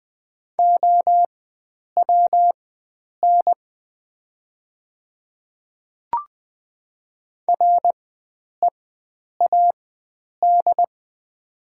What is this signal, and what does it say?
Morse code, heard as a keyed tone, about 21 wpm, spaced as for 9 wpm, 700 Hz. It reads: OWN READ
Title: 4x Character Spacing - Top 300 Words - 20wpm